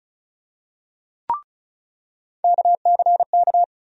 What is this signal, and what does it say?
Morse code, heard as a keyed tone, about 36 wpm, 700 Hz. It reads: KCK
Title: USA Call Sign Postfixes - 3 Letters 35wpm